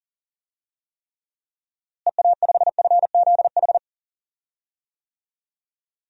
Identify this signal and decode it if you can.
Morse code, heard as a keyed tone, about 40 wpm, 700 Hz. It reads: EA5FZH
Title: DXCC-100 Call Signs - 6 Characters - Repeated 2x - 40wpm